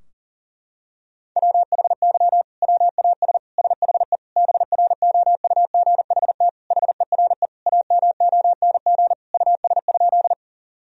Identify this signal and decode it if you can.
Morse code, heard as a keyed tone, about 40 wpm, 700 Hz. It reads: WHY WAS SHE BROUGHT HERE AMONG US?
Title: Sentences from Top 1000 Words - Review Part 3 - 40wpm